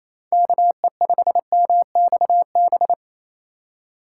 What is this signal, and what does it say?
Morse code, heard as a keyed tone, about 37 wpm, 700 Hz. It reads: KE5MXB